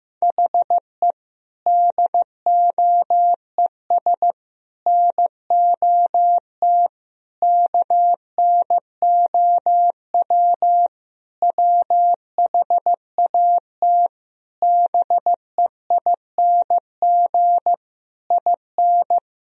Morse code, heard as a keyed tone, about 16 wpm, 700 Hz. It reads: HE DOES NOT KNOW WHAT BEING IN
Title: Sentences from Top 1000 Words - Review Part 3 - 15wpm